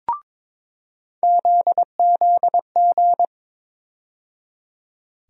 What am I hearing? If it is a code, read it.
Morse code, heard as a keyed tone, about 22 wpm, 700 Hz. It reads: ZZG
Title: USA Call Sign Postfixes - 3 Letters 22wpm